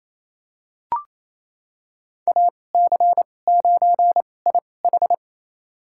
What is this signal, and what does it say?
Morse code, heard as a keyed tone, about 28 wpm, spaced as for 20 wpm, 700 Hz. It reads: AC9IH
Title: USA Call Signs - 5 Characters - 2x Character Spacing - 28wpm